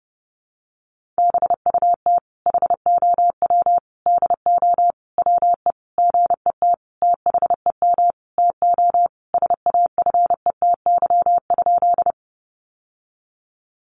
Morse code, heard as a keyed tone, about 30 wpm, 700 Hz. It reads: BUT HOW DO WE GET THEM TO SAFETY?
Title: Sentences from Top 700 Words 30wpm